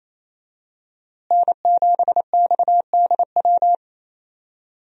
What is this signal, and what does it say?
Morse code, heard as a keyed tone, about 28 wpm, 700 Hz. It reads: N7XDW
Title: USA Call Signs - 5 Characters - Repeated 3x - 28wpm